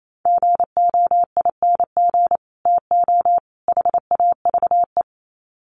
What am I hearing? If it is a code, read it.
Morse code, heard as a keyed tone, about 28 wpm, 700 Hz. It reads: GOING TO HAVE
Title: Sets of 3 Words 28wpm